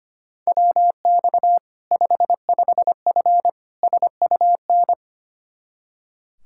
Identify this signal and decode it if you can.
Morse code, heard as a keyed tone, about 25 wpm, 700 Hz. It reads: WX 55F SUN
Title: K1USN SST QSO Practice 25wpm